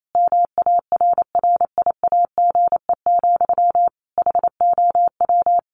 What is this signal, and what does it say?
Morse code, heard as a keyed tone, about 28 wpm, 700 Hz. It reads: MARRIAGE, HOW
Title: Sentences from Top 1100 Words 28wpm